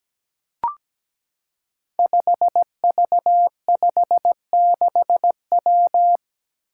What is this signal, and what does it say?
Morse code, heard as a keyed tone, about 17 wpm, 700 Hz. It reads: 5V56W
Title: Five Challenging Letter-Number Combinations 17wpm